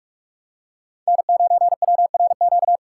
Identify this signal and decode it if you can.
Morse code, heard as a keyed tone, about 48 wpm, 700 Hz. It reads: N9WRQ
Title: USA Call Signs - 5 Characters - 4x Character Spacing - 45wpm